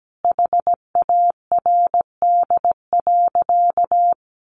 Morse code, heard as a keyed tone, about 17 wpm, 700 Hz. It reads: HARD.
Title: Sentences from Top 300 Words 17wpm